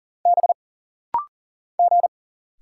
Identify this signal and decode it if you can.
Morse code, heard as a keyed tone, about 41 wpm, 700 Hz. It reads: B G
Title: Single Letter-Number Alt - Rapid-Fire 40wpm